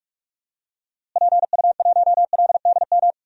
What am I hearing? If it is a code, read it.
Morse code, heard as a keyed tone, about 46 wpm, 700 Hz. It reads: PU1LDM
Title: DXCC-100 Call Signs - 6 Characters - Repeated 3x - 45wpm